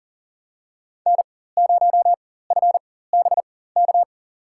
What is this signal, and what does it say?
Morse code, heard as a keyed tone, about 40 wpm, spaced as for 19 wpm, 700 Hz. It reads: N0FBK
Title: USA Call Signs - 4x Character Spacing - 40wpm